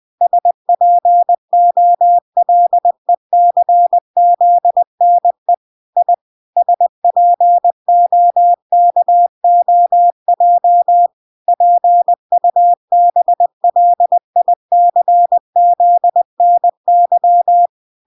Morse code, heard as a keyed tone, about 20 wpm, 700 Hz. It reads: SPOLECZNE I SPOKOJ PUBLICZNY